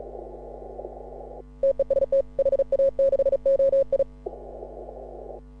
Morse code, heard as a keyed tone, about 36 wpm, 530 Hz. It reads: TEST HA6OI
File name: AUDIO2018 / HA6OI CW